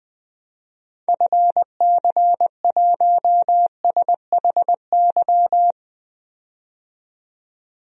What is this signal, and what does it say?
Morse code, heard as a keyed tone, about 20 wpm, 700 Hz. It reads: FC1SHY
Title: DXCC-100 Call Signs - 6 Characters - 20wpm